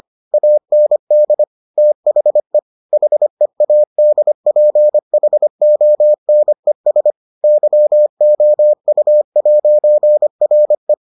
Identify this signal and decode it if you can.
Morse code, heard as a keyed tone, about 25 wpm, 600 Hz. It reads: AND THE HEADPHONES YOU'RE